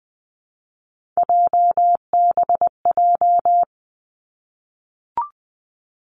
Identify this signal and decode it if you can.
Morse code, heard as a keyed tone, about 20 wpm, 700 Hz. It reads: JBJ
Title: Three Challenging Letter-Number Combinations 20wpm